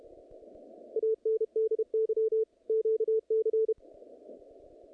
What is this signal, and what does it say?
Morse code, heard as a keyed tone, about 31 wpm, 440 Hz, with radio noise. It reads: ANDY QC
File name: contests / naqp-cw-k5tr-op-k5pi- -2200